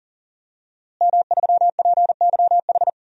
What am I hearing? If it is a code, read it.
Morse code, heard as a keyed tone, about 39 wpm, 700 Hz. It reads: M3PYH
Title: DXCC-100 Call Signs - 5 Characters - Repeated 3x - 40wpm